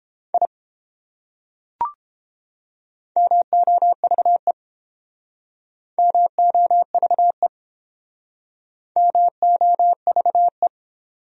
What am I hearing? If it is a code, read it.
Morse code, heard as a keyed tone, about 30 wpm, 700 Hz. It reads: I MOVE MOVE MOVE